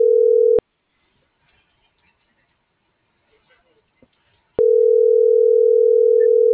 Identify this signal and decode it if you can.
Ringback tone (North America), 440 + 480 Hz.